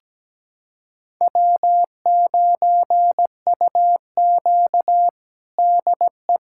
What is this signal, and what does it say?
Morse code, heard as a keyed tone, about 17 wpm, 700 Hz. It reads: W9UQ DE
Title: USA Callsign DE Callsign - 4 Characters - 17wpm